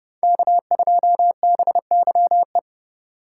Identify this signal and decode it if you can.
Morse code, heard as a keyed tone, about 31 wpm, 700 Hz. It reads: K2BYE
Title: Speed Racing - USA 1x3 Call Signs 25wpm